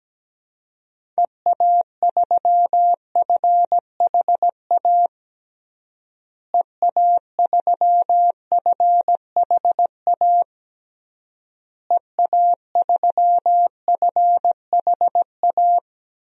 Morse code, heard as a keyed tone, about 17 wpm, 700 Hz. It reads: EA3FHA EA3FHA EA3FHA
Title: DXCC-100 Call Signs - 6 Characters - Repeated 3x - 17wpm